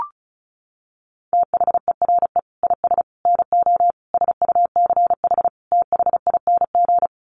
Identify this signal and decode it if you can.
Morse code, heard as a keyed tone, about 34 wpm, 700 Hz. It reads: THERE IS NO SUCH THING
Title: Sets of 5 Words Encore 35wpm